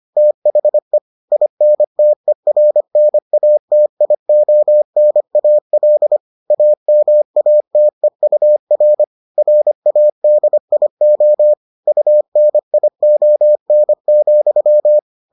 Morse code, heard as a keyed tone, about 25 wpm, 600 Hz. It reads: THE INTERNATIONAL AMATEUR RADIO UNION,